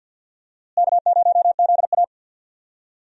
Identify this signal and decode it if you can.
Morse code, heard as a keyed tone, about 51 wpm, 700 Hz. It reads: K0ZA
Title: USA Call Signs - 4 Characters - Repeated 2x - 50wpm